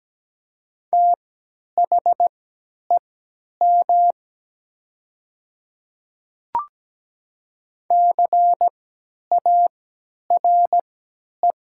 Morse code, heard as a keyed tone, about 17 wpm, spaced as for 10 wpm, 700 Hz. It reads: THEM CARE
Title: 3x Character Spacing - Top 300 Words - 17wpm